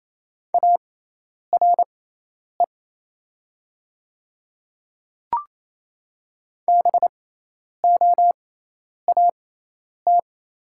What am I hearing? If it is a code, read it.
Morse code, heard as a keyed tone, about 28 wpm, spaced as for 10 wpm, 700 Hz. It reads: ARE BOAT
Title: 6x Character Spacing - Top 500 Words - 28wpm